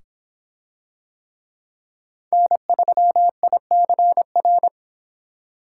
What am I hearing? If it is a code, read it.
Morse code, heard as a keyed tone, about 26 wpm, 700 Hz. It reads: N3ICR